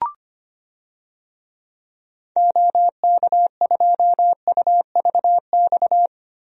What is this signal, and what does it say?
Morse code, heard as a keyed tone, about 25 wpm, 700 Hz. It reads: OK2UVX
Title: DXCC-100 Call Signs - 6 Characters - Repeated 3x - 25wpm